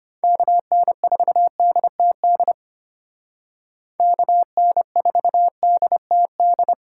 Morse code, heard as a keyed tone, about 33 wpm, 700 Hz. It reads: KN4DTD KN4DTD